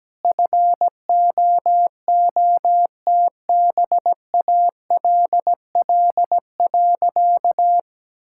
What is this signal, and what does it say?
Morse code, heard as a keyed tone, about 17 wpm, 700 Hz. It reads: FOOTBALL.
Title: Sentences from Top 1600 Words 17wpm